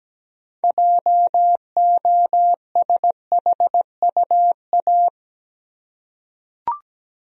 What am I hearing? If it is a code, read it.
Morse code, heard as a keyed tone, about 17 wpm, 700 Hz. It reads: JOSHUA
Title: USA Common Names - 17wpm